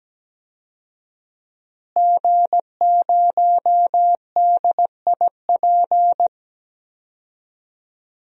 Morse code, heard as a keyed tone, about 17 wpm, 700 Hz. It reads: G0DIP
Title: DXCC-100 Call Signs - 5 Characters - Repeated 2x - 17wpm